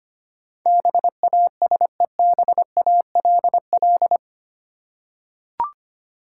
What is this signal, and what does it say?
Morse code, heard as a keyed tone, about 25 wpm, 690 Hz. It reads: BASEBALL
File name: 8 Letter Words - First 3 Letters Given - 25wpm